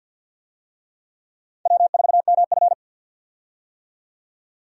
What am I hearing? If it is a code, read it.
Morse code, heard as a keyed tone, about 52 wpm, 700 Hz. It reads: W4MF
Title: USA Call Signs - 4 Characters - 50wpm